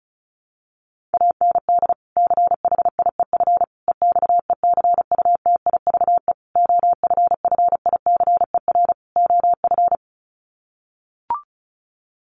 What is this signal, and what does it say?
Morse code, heard as a keyed tone, about 34 wpm, 700 Hz. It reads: AND CHIEF EXECUTIVE OFFICER OF
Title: Sets of 5 Words Encore2 35wpm